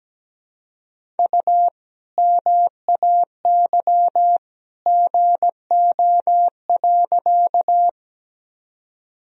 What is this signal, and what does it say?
Morse code, heard as a keyed tone, about 17 wpm, 700 Hz. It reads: U MAY GO.
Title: Sentences from Top 100 Words and Abbreviations 17wpm